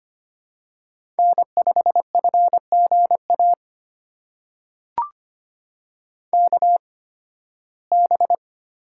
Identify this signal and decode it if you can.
Morse code, heard as a keyed tone, about 25 wpm, spaced as for 16 wpm, 700 Hz. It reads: N5FGA KB
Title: USA Call Signs - 8x Character Spacing - 25wpm